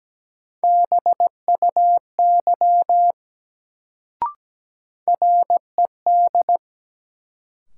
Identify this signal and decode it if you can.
Morse code, heard as a keyed tone, about 17 wpm, 700 Hz. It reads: BUY RED
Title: Sentences from Top 700 Words - New Word Drill - 17wpm